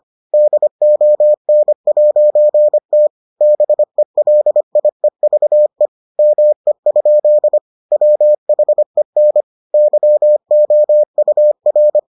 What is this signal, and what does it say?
Morse code, heard as a keyed tone, about 25 wpm, 600 Hz. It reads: DON'T BELIEVE ME? WHEN YOUR